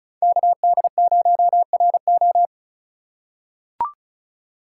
Morse code, heard as a keyed tone, about 36 wpm, 700 Hz. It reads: KD0RO
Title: USA Call Signs - 4x Character Spacing - 35wpm